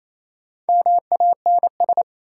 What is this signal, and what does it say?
Morse code, heard as a keyed tone, about 28 wpm, 700 Hz. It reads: MANS
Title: Sentences from Top 1000 Words - Review Part 6 - New Word Drill - 28wpm